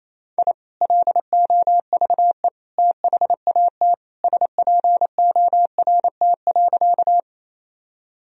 Morse code, heard as a keyed tone, about 27 wpm, 700 Hz. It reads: I LOVE THAT SPORT.
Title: Sentences from Top 1000 Words - Review Part 4 - 28wpm